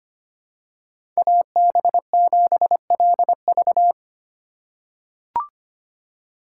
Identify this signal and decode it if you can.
Morse code, heard as a keyed tone, about 25 wpm, 700 Hz. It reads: AB7LV
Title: USA Call Signs - 8x Character Spacing - 25wpm